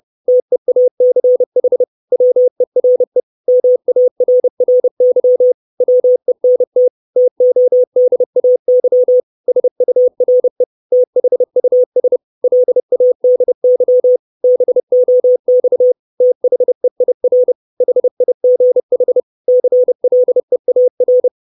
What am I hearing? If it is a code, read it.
Morse code, heard as a keyed tone, about 30 wpm, 500 Hz. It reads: TEACH WERE MARRY WENT TODAY SURE THUS LADY BOX THEIR HIGH CLEAR
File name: 632 Words 30WPM